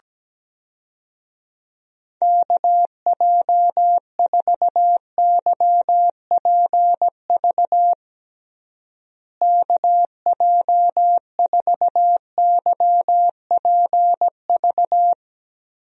Morse code, heard as a keyed tone, about 17 wpm, 700 Hz. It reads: KJ4YPV KJ4YPV
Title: USA Call Signs - 6 Characters - Repeated 3x - 17wpm